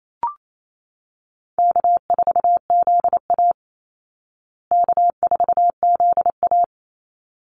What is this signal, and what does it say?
Morse code, heard as a keyed tone, about 28 wpm, 700 Hz. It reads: K4ZA K4ZA
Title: Easy Call Signs Repeated 2x - 28wpm